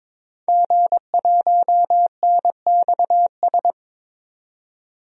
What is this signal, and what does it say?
Morse code, heard as a keyed tone, about 22 wpm, 700 Hz. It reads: G1NXS
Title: DXCC-100 Call Signs - 5 Characters - 22wpm